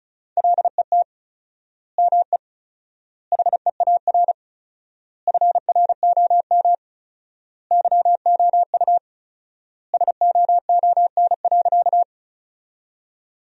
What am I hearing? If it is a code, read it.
Morse code, heard as a keyed tone, about 36 wpm, 700 Hz. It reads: LET ME HEAR FROM YOU SOON.
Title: 4x Word-Spacing - Sentences from Top 400 Words - 35wpm